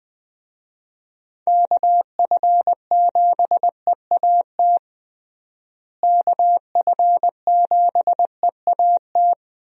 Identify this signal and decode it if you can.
Morse code, heard as a keyed tone, about 20 wpm, 700 Hz. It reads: KF7EAT KF7EAT
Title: USA Call Signs - 6 Characters - Repeated 3x - 20wpm